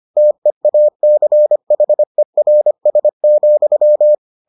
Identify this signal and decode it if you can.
Morse code, heard as a keyed tone, about 25 wpm, 600 Hz. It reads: TEACHERS,